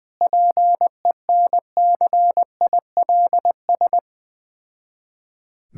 Morse code, heard as a keyed tone, about 20 wpm, 690 Hz. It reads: PENCILS
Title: Difficult Words with Hints - 20wpm